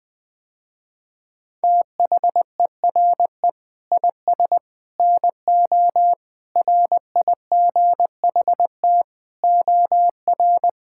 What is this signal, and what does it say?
Morse code, heard as a keyed tone, about 20 wpm, 700 Hz. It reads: THERE IS NO RIGHT OR
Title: Sentences with 8-Word Phrases - Spoken Beginning - 20wpm